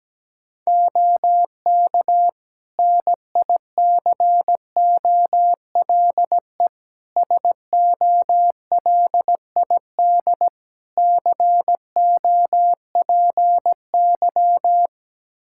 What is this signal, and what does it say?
Morse code, heard as a keyed tone, about 17 wpm, 700 Hz. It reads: OK NICOLE SOLID COPY
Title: Practice QSO - Exchange 3 - More Info - 17wpm